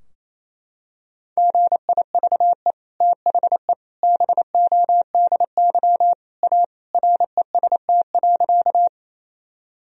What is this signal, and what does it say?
Morse code, heard as a keyed tone, about 27 wpm, 700 Hz. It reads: GIVE THE BODY A REST.
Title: Sentences from Top 1000 Words - Review Part 2 - 28wpm